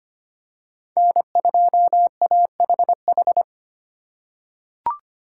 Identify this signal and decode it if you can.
Morse code, heard as a keyed tone, about 25 wpm, 700 Hz. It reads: N2AHH
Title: USA Call Signs - 2x Character Spacing - 25wpm